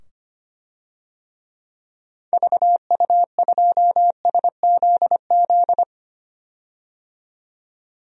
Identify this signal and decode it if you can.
Morse code, heard as a keyed tone, about 25 wpm, 700 Hz. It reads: VU2SZZ
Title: DXCC-100 Call Signs - 6 Characters - 25wpm